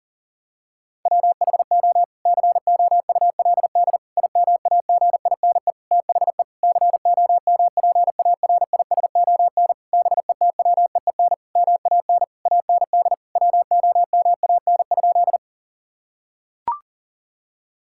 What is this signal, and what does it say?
Morse code, heard as a keyed tone, about 37 wpm, 700 Hz. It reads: WHO COULD IMAGINE THE COMPARISON BETWEEN MAN AND WOMAN?
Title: Sentences from Top 1600 Words 40wpm